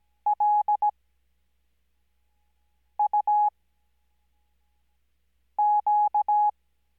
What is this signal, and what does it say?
Morse code, heard as a keyed tone, about 17 wpm, spaced as for 4 wpm, 840 Hz. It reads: LUQ
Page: Farnsworth Spacing: 15WPM Character speed Spaced at 5WPM.